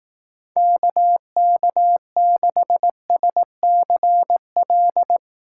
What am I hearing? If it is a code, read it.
Morse code, heard as a keyed tone, about 18 wpm, 700 Hz. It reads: KK6SCL
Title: Speed Racing - USA 2x3 Call Signs 15wpm